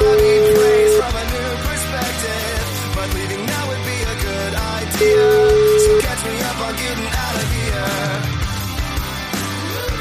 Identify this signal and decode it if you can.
Ringback tone (Europe, 425 Hz), with noise.